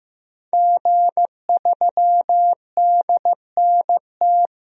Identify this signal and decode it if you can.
Morse code, heard as a keyed tone, about 15 wpm, 700 Hz. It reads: G3DNT
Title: DXCC-100 Call Signs - 5 Characters - Repeated 3x - 15wpm